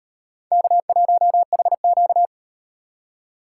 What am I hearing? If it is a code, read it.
Morse code, heard as a keyed tone, about 34 wpm, 700 Hz. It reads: K1HQ